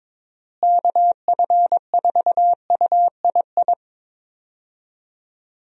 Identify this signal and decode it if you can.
Morse code, heard as a keyed tone, about 22 wpm, 700 Hz. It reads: KF4UII
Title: USA Call Signs - 6 Characters - Repeated 2x - 22wpm